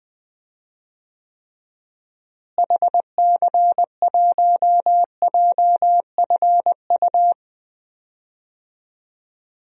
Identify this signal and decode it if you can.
Morse code, heard as a keyed tone, about 20 wpm, 700 Hz. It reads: HC1JFU
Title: DXCC-100 Call Signs - 6 Characters - Repeated 3x - 20wpm